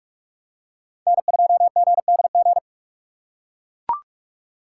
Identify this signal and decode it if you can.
Morse code, heard as a keyed tone, about 40 wpm, 700 Hz. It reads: N2GDG